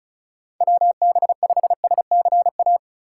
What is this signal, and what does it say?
Morse code, heard as a keyed tone, about 36 wpm, 700 Hz. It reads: WB5SCA
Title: USA Call Signs - 6 Characters - 2x Character Spacing - 35wpm